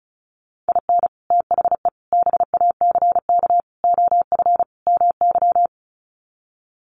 Morse code, heard as a keyed tone, about 34 wpm, 700 Hz. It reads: IN THE BACK OF MY
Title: Sets of 5 Words Encore 35wpm